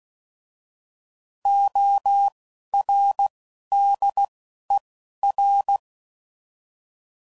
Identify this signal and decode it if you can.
Morse code, heard as a keyed tone, about 16 wpm, spaced as for 11 wpm, 800 Hz. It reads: ORDER
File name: Prefixes / By First Letter / Os